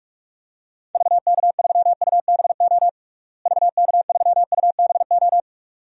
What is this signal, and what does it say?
Morse code, heard as a keyed tone, about 42 wpm, 700 Hz. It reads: VK3UBO VK3UBO
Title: DXCC-100 Call Signs - 6 Characters - Repeated 2x - 45wpm